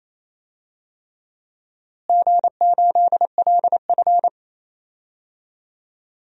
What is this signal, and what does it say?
Morse code, heard as a keyed tone, about 28 wpm, 700 Hz. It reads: G8LF
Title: DXCC-100 Call Signs - 4 Characters - Repeated 3x - 28wpm